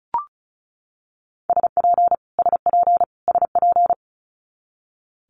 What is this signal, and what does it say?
Morse code, heard as a keyed tone, about 34 wpm, 700 Hz. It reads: SP SP SP
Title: Multi-letter Phonemes Repeated 3x 35wpm